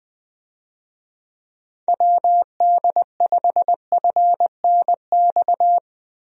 Morse code, heard as a keyed tone, about 20 wpm, 700 Hz. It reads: WD5FNX
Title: USA Call Signs - 6 Characters - Repeated 2x - 20wpm